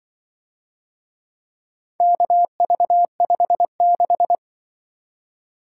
Morse code, heard as a keyed tone, about 25 wpm, 700 Hz. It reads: KV56